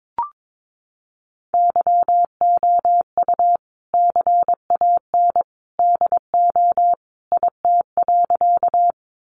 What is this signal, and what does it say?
Morse code, heard as a keyed tone, about 22 wpm, 700 Hz. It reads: YOU CAN DO IT.
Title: Sentences from Top 100 Words 22wpm